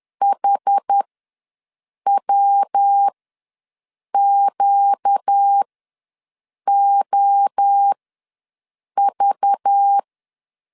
Morse code, heard as a keyed tone, about 10 wpm, spaced as for 6 wpm, 800 Hz. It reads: HWQOV
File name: LEZIONI / 5 LBVC / 5 LBV ws 10wpm 3645